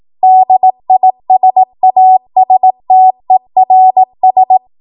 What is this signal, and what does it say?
Morse code, heard as a keyed tone, about 18 wpm, 750 Hz. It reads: DISASTERS
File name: Morse / 18WPM